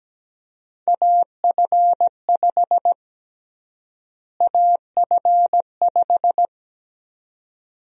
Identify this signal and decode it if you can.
Morse code, heard as a keyed tone, about 17 wpm, 700 Hz. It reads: AF5 AF5
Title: USA Call Sign Prefixes - Repeated 2x - 17wpm